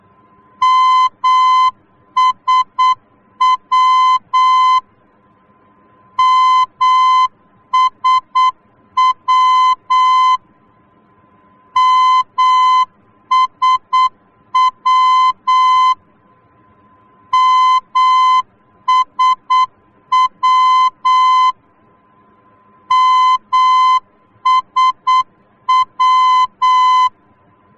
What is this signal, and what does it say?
Morse code, heard as a keyed tone, about 8 wpm, 1020 Hz. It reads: MSW MSW MSW MSW MSW
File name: germany / mx / MSW-D